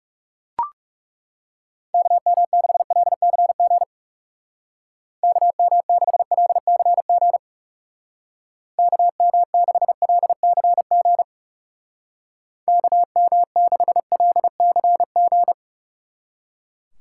Morse code, heard as a keyed tone, about 37 wpm, 700 Hz. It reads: KM6LCG KM6LCG KM6LCG KM6LCG
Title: Speed Racing - USA 2x3 Call Signs 30wpm